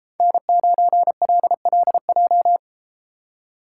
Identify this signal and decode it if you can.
Morse code, heard as a keyed tone, about 33 wpm, 700 Hz. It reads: N9LLJ